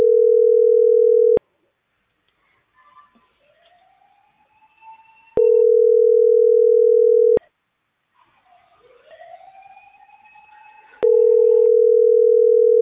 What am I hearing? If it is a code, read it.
Ringback tone (North America), 440 + 480 Hz.